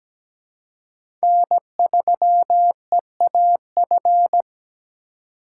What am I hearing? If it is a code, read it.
Morse code, heard as a keyed tone, about 17 wpm, 700 Hz. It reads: N3EAF
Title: USA Call Signs - 5 Characters - 4x Character Spacing - 17wpm